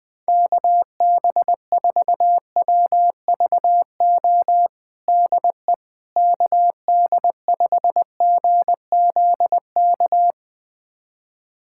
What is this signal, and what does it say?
Morse code, heard as a keyed tone, about 20 wpm, 700 Hz. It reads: KB4WVO DE KD5GZK
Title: USA Callsign DE Callsign 20wpm